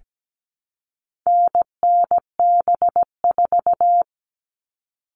Morse code, heard as a keyed tone, about 17 wpm, 700 Hz. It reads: NNB4
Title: All Cut Numbers - 4-Digit 17wpm